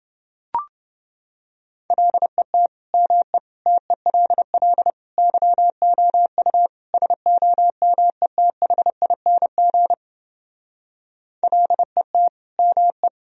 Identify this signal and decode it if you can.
Morse code, heard as a keyed tone, about 30 wpm, 700 Hz. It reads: LET ME TELL YOU SOMETHING LET ME